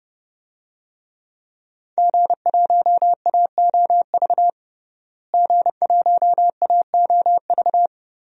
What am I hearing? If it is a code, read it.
Morse code, heard as a keyed tone, about 30 wpm, 700 Hz. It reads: G1AOV G1AOV
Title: DXCC-100 Call Signs - 5 Characters - Repeated 3x - 30wpm